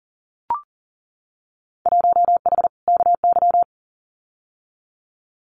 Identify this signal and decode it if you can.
Morse code, heard as a keyed tone, about 40 wpm, 700 Hz. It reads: 1H KY
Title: Winter Field Day Exchanges 40wpm